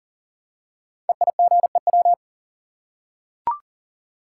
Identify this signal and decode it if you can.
Morse code, heard as a keyed tone, about 38 wpm, 690 Hz. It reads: EIGEW
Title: Code Groups - 5 Letters - 40wpm